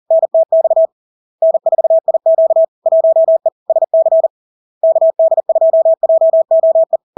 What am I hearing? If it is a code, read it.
Morse code, heard as a keyed tone, about 40 wpm, 650 Hz. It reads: NTX N4IQ 1E SC KD2JOE
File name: LARK-FD-2022-Log-in-CW-40wpm-650-Hz